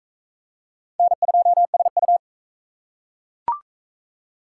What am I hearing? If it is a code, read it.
Morse code, heard as a keyed tone, about 38 wpm, 700 Hz. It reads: N2SU